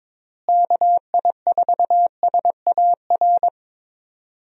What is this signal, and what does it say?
Morse code, heard as a keyed tone, about 22 wpm, 700 Hz. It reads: KI4SAR
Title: USA Call Signs - 6 Characters - Repeated 3x - 22wpm